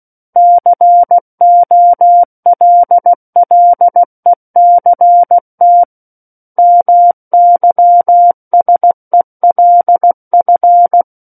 Morse code, heard as a keyed tone, about 16 wpm, 700 Hz. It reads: COLLECT MYSELF
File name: Book One / 07-Chapter 7 - 16 WPM